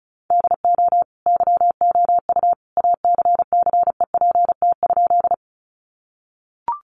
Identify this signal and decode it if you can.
Morse code, heard as a keyed tone, about 34 wpm, 700 Hz. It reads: DO YOU ACCEPT?
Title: Sentences from Top 1300 Words 35wpm